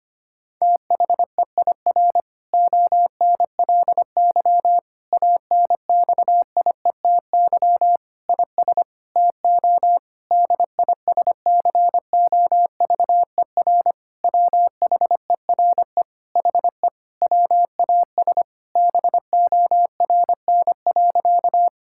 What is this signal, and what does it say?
Morse code, heard as a keyed tone, about 25 wpm, 700 Hz. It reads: THEIR ONLY ANXIETY IS TO DISCOVER WHERE HE WAS BORN.
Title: Sentences from Top 1600 Words 25wpm